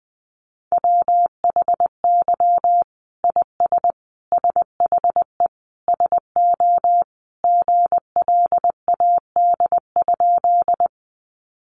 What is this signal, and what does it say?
Morse code, heard as a keyed tone, about 20 wpm, 700 Hz. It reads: WHY IS SHE SO GLAD?
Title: Sentences from Top 800 Words 20wpm